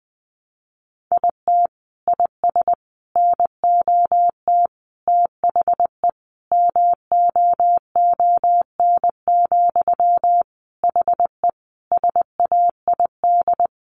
Morse code, heard as a keyed tone, about 20 wpm, 700 Hz. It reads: IT IS NOT THE MOON, HE SAID
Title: Sentences from Top 1100 Words 20wpm